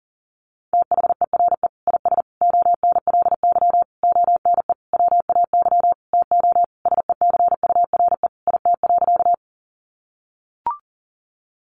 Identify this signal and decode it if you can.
Morse code, heard as a keyed tone, about 40 wpm, 700 Hz. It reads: THERE IS ONLY ONE WAY TO SECURE IT.
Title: Sentences from Top 1200 Words 40wpm